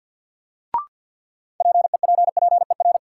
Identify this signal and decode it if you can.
Morse code, heard as a keyed tone, about 53 wpm, 700 Hz. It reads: PEPPER
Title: Sets of 5 Words - New Word Drill - 50wpm